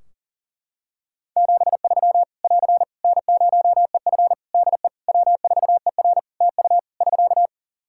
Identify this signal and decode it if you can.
Morse code, heard as a keyed tone, about 44 wpm, 700 Hz. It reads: 73 <AR> N0EF DE W4ER TU <SK>
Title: Practice QSO - Exchange 4 - Ending - 40wpm